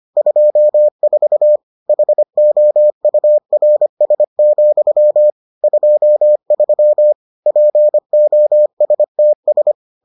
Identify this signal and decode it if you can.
Morse code, heard as a keyed tone, about 25 wpm, 600 Hz. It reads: 24 HOURS, 23 POSTS